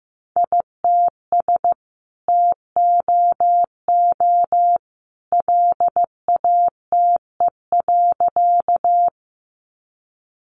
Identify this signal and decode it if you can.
Morse code, heard as a keyed tone, about 15 wpm, 700 Hz. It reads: ITS TOO LATE.
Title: Sentences from Top 600 Words 15wpm